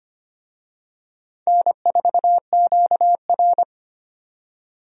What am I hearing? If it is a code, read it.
Morse code, heard as a keyed tone, about 25 wpm, 700 Hz. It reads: N4QR
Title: USA Call Signs - 4 Characters - 25wpm